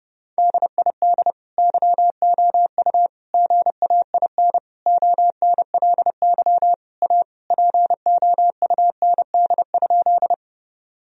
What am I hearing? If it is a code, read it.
Morse code, heard as a keyed tone, about 30 wpm, 700 Hz. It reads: DID YOU GAIN ONLY A POUND?
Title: Sentences from Top 1000 Words - Review Part 2 - 30wpm